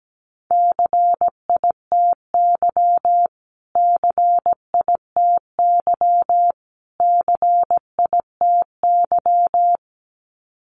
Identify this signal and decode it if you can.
Morse code, heard as a keyed tone, about 17 wpm, 700 Hz. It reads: CITY CITY CITY
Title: Top 500 Words Repeated 17wpm